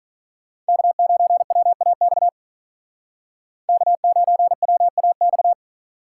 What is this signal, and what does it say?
Morse code, heard as a keyed tone, about 45 wpm, 700 Hz. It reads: K9WAX K9WAX